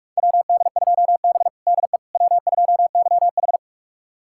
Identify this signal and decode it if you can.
Morse code, heard as a keyed tone, about 46 wpm, 700 Hz. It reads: WD2B DE W2YH
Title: USA Callsign DE Callsign - 4 Characters - 45wpm